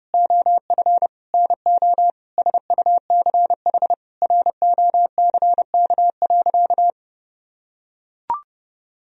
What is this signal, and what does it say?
Morse code, heard as a keyed tone, about 30 wpm, 700 Hz. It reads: OF NO SUCH ROCK.
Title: Sentences from Top 1000 Words - Review Part 2 - 30wpm